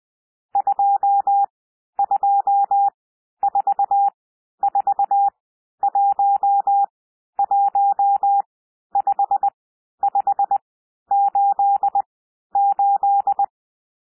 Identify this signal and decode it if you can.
Morse code, heard as a keyed tone, about 20 wpm, spaced as for 11 wpm, 800 Hz. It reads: 2244115588
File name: cwmp3 / N1 SOUND NUMBERS 20 wpm